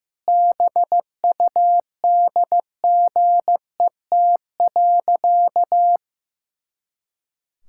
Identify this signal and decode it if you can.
Morse code, heard as a keyed tone, about 15 wpm, 700 Hz. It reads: BUDGET.
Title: Pangram Challenge v2 - 15wpm